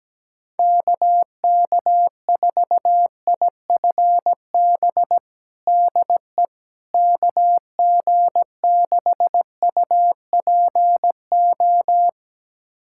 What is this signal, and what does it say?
Morse code, heard as a keyed tone, about 17 wpm, 700 Hz. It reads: KK4IFB DE KG6UPO
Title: USA Callsign DE Callsign - 6 Characters - 17wpm